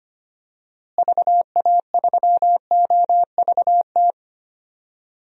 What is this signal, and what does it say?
Morse code, heard as a keyed tone, about 25 wpm, 700 Hz. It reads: VA3OVT
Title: DXCC-100 Call Signs - 6 Characters - Repeated 3x - 25wpm